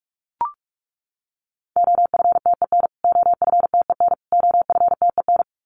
Morse code, heard as a keyed tone, about 44 wpm, 700 Hz. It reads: OFTEN OFTEN OFTEN
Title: Top 300 Words Repeated 45wpm